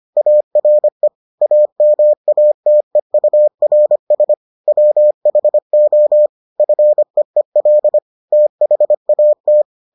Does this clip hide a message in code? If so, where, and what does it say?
Yes: Morse code from 0.2 to 9.6 s, heard as a keyed tone, about 25 wpm, 600 Hz. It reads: ARE AMATEURS WHO FEEL THAT